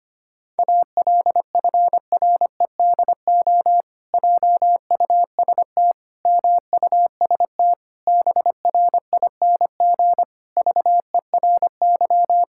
Morse code, heard as a keyed tone, about 25 wpm, 700 Hz. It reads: ALFREDO JUST MUST BRING VERY
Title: Pangram Challenge v2 - 25wpm